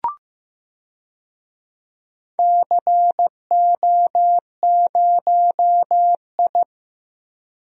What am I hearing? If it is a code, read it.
Morse code, heard as a keyed tone, about 16 wpm, 700 Hz. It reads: CO0I
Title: DXCC-100 Call Signs - 4 Characters - 15wpm